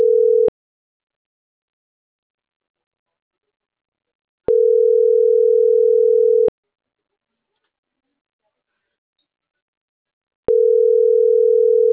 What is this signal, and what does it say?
Ringback tone (North America), 440 + 480 Hz.